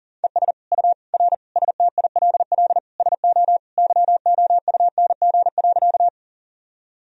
Morse code, heard as a keyed tone, about 40 wpm, 700 Hz. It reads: ES U R STILL SO YOUNG.
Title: Sentences from Top 400 Words and Abbreviations 40wpm